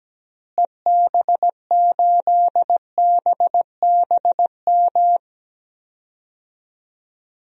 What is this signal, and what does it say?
Morse code, heard as a keyed tone, about 17 wpm, 700 Hz. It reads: EB8BBM
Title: DXCC-100 Call Signs - 6 Characters - Repeated 2x - 17wpm